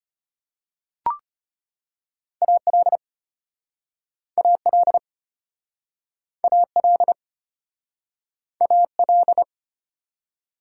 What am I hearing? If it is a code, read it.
Morse code, heard as a keyed tone, about 33 wpm, 700 Hz. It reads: AL AL AL AL